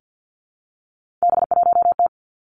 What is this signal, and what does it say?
Morse code, heard as a keyed tone, about 50 wpm, 700 Hz. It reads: B1T
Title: Three Challenging Letter-Number Combinations 50wpm